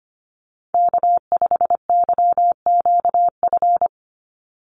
Morse code, heard as a keyed tone, about 25 wpm, 700 Hz. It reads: K5YQF
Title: Call Signs - 25wpm